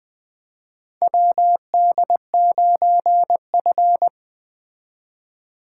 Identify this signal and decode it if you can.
Morse code, heard as a keyed tone, about 20 wpm, 700 Hz. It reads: WD9F
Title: 2023 SOTA Chaser Call Signs - 20wpm